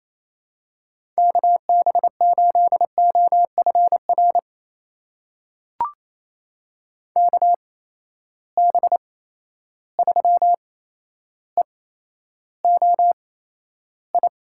Morse code, heard as a keyed tone, about 28 wpm, spaced as for 12 wpm, 700 Hz. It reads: KB8OFR KB3EOI
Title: USA Call Signs - 6 Characters - 8x Character Spacing - 28wpm